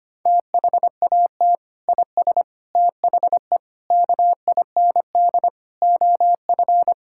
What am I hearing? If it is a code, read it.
Morse code, heard as a keyed tone, about 25 wpm, 700 Hz. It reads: THAT IS THE KIND OF
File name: practice / 25wpm / Sentences with 3-Word Phrases - 25wpm